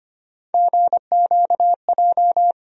Morse code, heard as a keyed tone, about 25 wpm, 700 Hz. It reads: GQJ
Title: USA Call Sign Postfixes - 3 Letters 25wpm